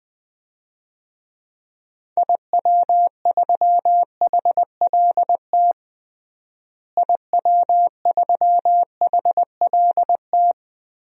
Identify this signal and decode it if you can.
Morse code, heard as a keyed tone, about 21 wpm, 700 Hz. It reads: IW3HLT IW3HLT
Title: DXCC-100 Call Signs - 6 Characters - Repeated 3x - 20wpm